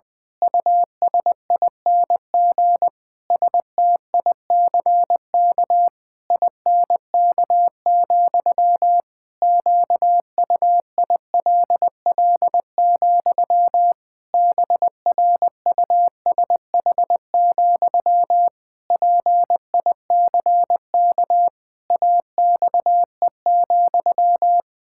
Morse code, heard as a keyed tone, about 21 wpm, 700 Hz. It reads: USING STICK INK, QUILL, BRUSH, PICK AXE,
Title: Pangram Challenge v2 - 20wpm